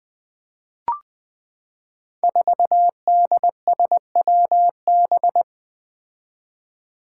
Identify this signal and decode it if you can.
Morse code, heard as a keyed tone, about 20 wpm, 700 Hz. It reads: 4DSWB
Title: Five Challenging Letter-Number Combinations 20wpm